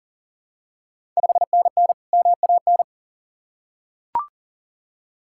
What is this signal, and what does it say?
Morse code, heard as a keyed tone, about 38 wpm, 700 Hz. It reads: 5NN MAN
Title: Ontario QSO Party Exchange 40wpm